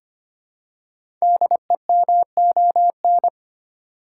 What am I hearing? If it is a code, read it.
Morse code, heard as a keyed tone, about 25 wpm, 700 Hz. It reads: DEMON
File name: 5 Letter Words - First 3 Letters Given - 25wpm